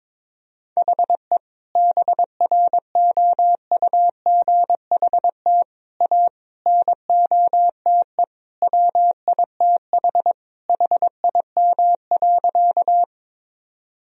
Morse code, heard as a keyed tone, about 22 wpm, 700 Hz. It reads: HE BROUGHT A NOTE WITH HIM.
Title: Sentences from Top 1000 Words - Review Part 2 - 22wpm